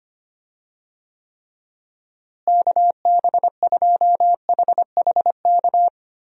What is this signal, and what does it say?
Morse code, heard as a keyed tone, about 25 wpm, 700 Hz. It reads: KB2HHK
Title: USA Call Signs - 6 Characters - Repeated 2x - 25wpm